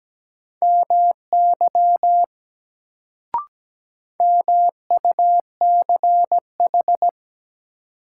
Morse code, heard as a keyed tone, about 17 wpm, 700 Hz. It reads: MY MUCH
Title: Sentences from Top 200 Words - New Word Drill - 17wpm